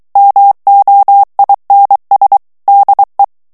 Morse code, heard as a keyed tone, about 23 wpm, 800 Hz. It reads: MOINS DE
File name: FRANCAIS TEXTE CLAIR 1 VITESSE 22 MOTS MINUTES